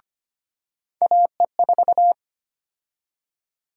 Morse code, heard as a keyed tone, about 25 wpm, 700 Hz. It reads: AE4
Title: USA Call Sign Prefixes - 25wpm